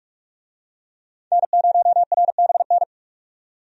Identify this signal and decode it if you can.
Morse code, heard as a keyed tone, about 47 wpm, 700 Hz. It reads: N0RBN
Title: USA Call Signs - 5 Characters - 45wpm